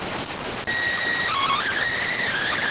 SSTV (slow-scan television), PD 290.